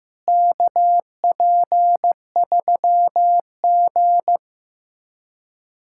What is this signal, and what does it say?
Morse code, heard as a keyed tone, about 15 wpm, 700 Hz. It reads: KP3G
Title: USA Callsign DE Callsign - 4 Characters - 15wpm